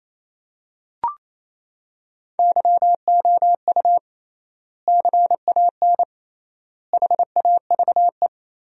Morse code, heard as a keyed tone, about 28 wpm, 700 Hz. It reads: YOU CAN HAVE
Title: 3x Word-Spacing - Sentences from Top 100 Words - 28wpm